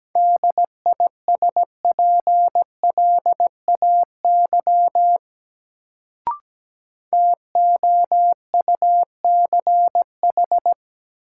Morse code, heard as a keyed tone, about 17 wpm, 700 Hz. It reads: DISPLAY TOUCH
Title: Sentences from Top 1200 Words - New Word Drill Alt - 17wpm